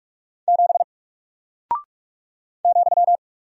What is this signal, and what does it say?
Morse code, heard as a keyed tone, about 46 wpm, 700 Hz. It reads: 7 ,
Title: Single Character (Letter:Number:Punctuation:Prosign) Alt 45wpm